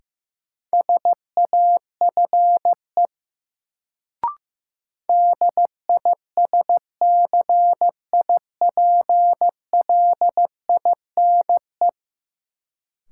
Morse code, heard as a keyed tone, about 15 wpm, 700 Hz. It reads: SAFE DISCIPLINE
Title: Sentences from Top 900 Words - New Word Drill - 15wpm